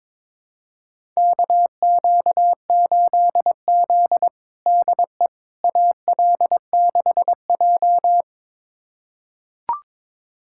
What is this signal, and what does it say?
Morse code, heard as a keyed tone, about 22 wpm, 700 Hz. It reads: KQ8Z DE AL6J
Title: USA Callsign DE Callsign - 4 Characters - 22wpm